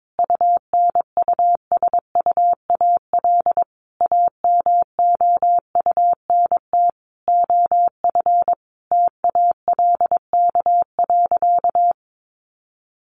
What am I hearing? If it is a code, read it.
Morse code, heard as a keyed tone, about 22 wpm, 700 Hz. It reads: UNUSUAL AMOUNT OF TALK.
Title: Sentences from Top 1400 Words 22wpm